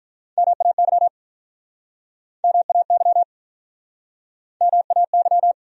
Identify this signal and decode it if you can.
Morse code, heard as a keyed tone, about 49 wpm, 700 Hz. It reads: MAY MAY MAY